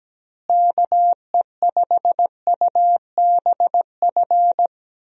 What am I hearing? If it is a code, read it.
Morse code, heard as a keyed tone, about 17 wpm, 700 Hz. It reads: KE5UBF
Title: USA Call Signs - 6 Characters - Repeated 2x - 17wpm